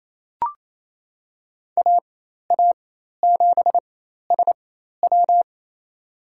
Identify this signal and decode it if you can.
Morse code, heard as a keyed tone, about 28 wpm, spaced as for 13 wpm, 700 Hz. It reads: AA7SW
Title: USA Call Signs - 4x Character Spacing - 28wpm